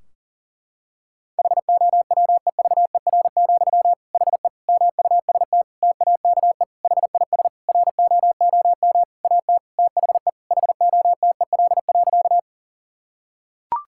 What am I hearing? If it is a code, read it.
Morse code, heard as a keyed tone, about 37 wpm, 700 Hz. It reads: HOWEVER, HE MUST TAKE HIS ROOM AT THE HOTEL.
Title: Sentences from Top 1600 Words 40wpm